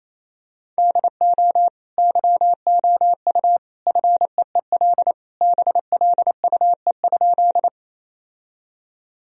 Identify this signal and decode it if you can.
Morse code, heard as a keyed tone, about 28 wpm, 700 Hz. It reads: DO YOU FEEL BLUE?
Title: Sentences from Top 1000 Words - Review Part 3 - 28wpm